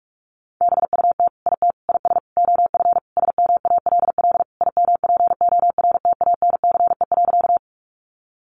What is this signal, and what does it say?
Morse code, heard as a keyed tone, about 44 wpm, 700 Hz. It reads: BUT IT IS OF SMALL IMPORTANCE.
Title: Sentences from Top 1200 Words 45wpm